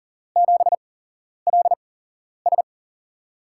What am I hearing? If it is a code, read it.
Morse code, heard as a keyed tone, about 40 wpm, spaced as for 11 wpm, 700 Hz. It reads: 7LS
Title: USA Call Signs - 4 Characters - 8x Character Spacing - 40wpm